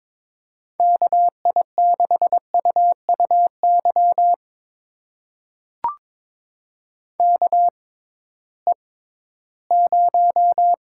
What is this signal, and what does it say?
Morse code, heard as a keyed tone, about 22 wpm, spaced as for 14 wpm, 700 Hz. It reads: KI6UUY KE0
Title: USA Call Signs - 6 Characters - 6x Character Spacing - 22wpm